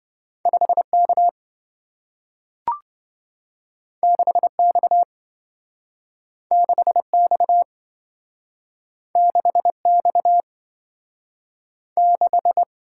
Morse code, heard as a keyed tone, about 27 wpm, 700 Hz. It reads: 5K 6X 6X 6X 6